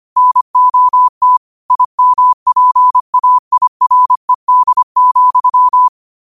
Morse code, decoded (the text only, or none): NOT IMPAIRED,